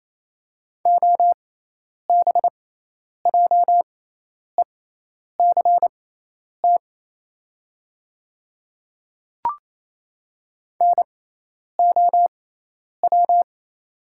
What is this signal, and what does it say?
Morse code, heard as a keyed tone, about 28 wpm, spaced as for 10 wpm, 690 Hz. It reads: OBJECT NOW
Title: 6x Character Spacing - Top 500 Words - 28wpm